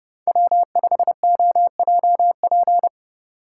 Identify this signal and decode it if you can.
Morse code, heard as a keyed tone, about 30 wpm, 700 Hz. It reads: W5OJP
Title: USA Call Signs - 8x Character Spacing - 30wpm